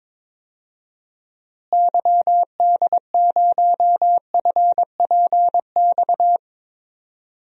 Morse code, heard as a keyed tone, about 22 wpm, 700 Hz. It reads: YD0FPX
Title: DXCC-100 Call Signs - 6 Characters - Repeated 3x - 22wpm